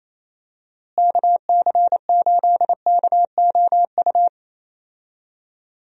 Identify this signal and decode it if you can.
Morse code, heard as a keyed tone, about 28 wpm, 700 Hz. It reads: KC8KOU
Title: USA Call Signs - 6 Characters - Repeated 3x - 28wpm